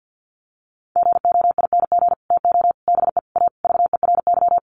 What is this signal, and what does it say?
Morse code, heard as a keyed tone, about 50 wpm, 700 Hz. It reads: GOING TO BE A VERY
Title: Sets of 5 Words Encore 50wpm